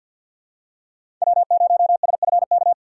Morse code, heard as a keyed tone, about 51 wpm, 700 Hz. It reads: W0SFK
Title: USA Call Signs - 5 Characters - 2x Character Spacing - 50wpm